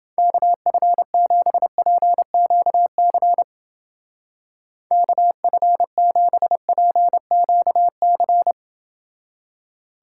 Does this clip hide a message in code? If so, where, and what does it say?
Yes: Morse code from 0.2 to 8.5 s, heard as a keyed tone, about 29 wpm, 700 Hz. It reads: KF7PQC KF7PQC